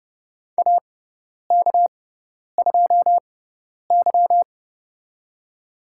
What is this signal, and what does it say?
Morse code, heard as a keyed tone, about 30 wpm, spaced as for 10 wpm, 700 Hz. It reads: AK2Y
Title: USA Call Signs - 4 Characters - 6x Character Spacing - 30wpm